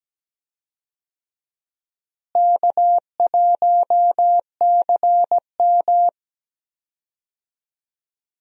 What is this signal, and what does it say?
Morse code, heard as a keyed tone, about 17 wpm, 700 Hz. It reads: K1CM
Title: USA Call Signs - 4 Characters - Repeated 2x - 17wpm